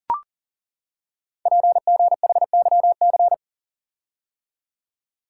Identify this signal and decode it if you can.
Morse code, heard as a keyed tone, about 41 wpm, 700 Hz. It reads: PGHYC
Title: Code Groups - 5 Letters - 40wpm